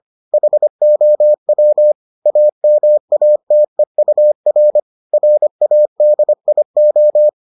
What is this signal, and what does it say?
Morse code, heard as a keyed tone, about 25 wpm, 600 Hz. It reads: HOW AMATEUR RADIO